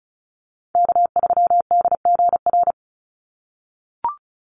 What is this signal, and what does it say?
Morse code, heard as a keyed tone, about 34 wpm, 700 Hz. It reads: K3DGR
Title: Call Signs - 35wpm